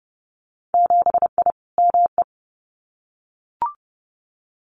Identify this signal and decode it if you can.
Morse code, heard as a keyed tone, about 30 wpm, 700 Hz. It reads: 7I ME
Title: Winter Field Day Exchanges Encore 30wpm